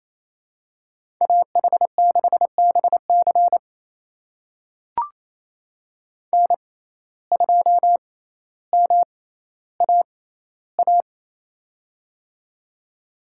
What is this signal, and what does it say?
Morse code, heard as a keyed tone, about 28 wpm, spaced as for 14 wpm, 700 Hz. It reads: AH6BC N2MAA
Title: USA Call Signs - 5 Characters - 6x Character Spacing - 28wpm